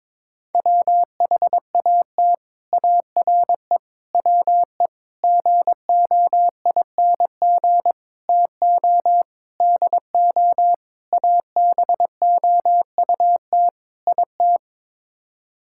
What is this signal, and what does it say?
Morse code, heard as a keyed tone, about 22 wpm, 700 Hz. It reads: WHAT ARE WE GOING TO DO ABOUT IT
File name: practice / 22wpm / Sets of 8 Words - 22wpm